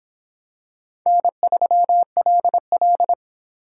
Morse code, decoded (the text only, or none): N3LL